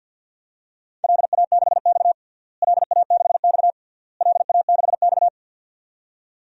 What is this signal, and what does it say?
Morse code, heard as a keyed tone, about 53 wpm, 700 Hz. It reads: LA6X LA6X LA6X
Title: DXCC-100 Call Signs - 4 Characters - Repeated 3x - 50wpm